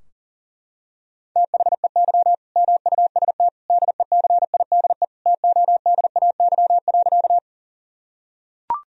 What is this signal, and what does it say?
Morse code, heard as a keyed tone, about 40 wpm, 700 Hz. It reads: THEY MUST DECIDE TODAY.
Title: Sentences from Top 1000 Words - Review Part 3 - 40wpm